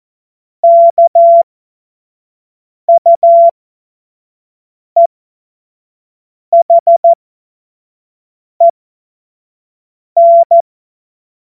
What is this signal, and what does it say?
Morse code, heard as a keyed tone, about 14 wpm, spaced as for 5 wpm, 680 Hz. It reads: KUEHEN
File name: Morsekurs / Witze / Witz 07. September 2021, 25 F70